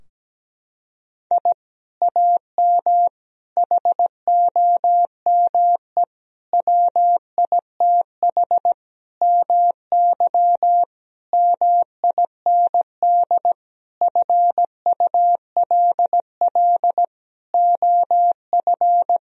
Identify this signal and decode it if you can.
Morse code, heard as a keyed tone, about 17 wpm, 700 Hz. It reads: I AM HOME WITH MY MIND FULL OF
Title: Sentences from Top 1000 Words - Review Part 3 - 17wpm